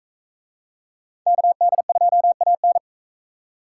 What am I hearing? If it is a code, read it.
Morse code, heard as a keyed tone, about 44 wpm, 700 Hz. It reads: KD2AN